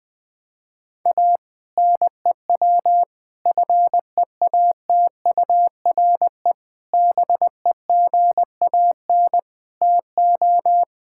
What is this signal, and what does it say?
Morse code, heard as a keyed tone, about 20 wpm, 700 Hz. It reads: A NEW FEATURE BEGAN TO
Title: Sentences from Top 1000 Words - Review Part 5 - 20wpm